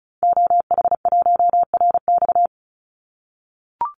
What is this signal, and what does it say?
Morse code, heard as a keyed tone, about 34 wpm, 700 Hz. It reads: OH1RX
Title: Call Signs - 35wpm